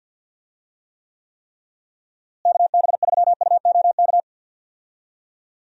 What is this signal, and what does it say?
Morse code, heard as a keyed tone, about 50 wpm, 700 Hz. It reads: KB3UOK
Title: USA Call Signs - 6 Characters - Repeated 3x - 50wpm